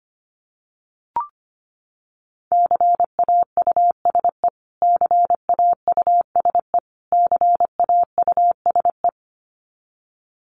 Morse code, heard as a keyed tone, about 25 wpm, 700 Hz. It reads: CAUSE CAUSE CAUSE
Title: Top 200 Words Repeated 25wpm